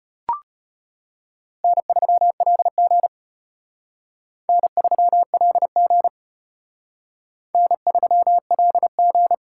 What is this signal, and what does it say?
Morse code, heard as a keyed tone, about 33 wpm, 700 Hz. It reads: N3LG N3LG N3LG